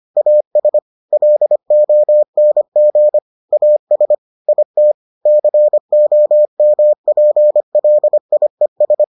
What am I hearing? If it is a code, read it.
Morse code, heard as a keyed tone, about 25 wpm, 600 Hz. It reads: AS LONG AS IT COMPLIES